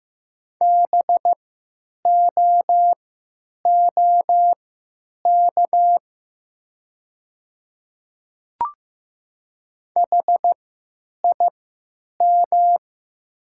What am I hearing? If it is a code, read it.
Morse code, heard as a keyed tone, about 15 wpm, spaced as for 9 wpm, 700 Hz. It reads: BOOK HIM
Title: 3x Character Spacing - Top 300 Words - 15wpm